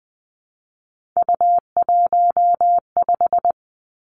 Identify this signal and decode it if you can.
Morse code, heard as a keyed tone, about 20 wpm, 700 Hz. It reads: U15
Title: Three Challenging Letter-Number Combinations 20wpm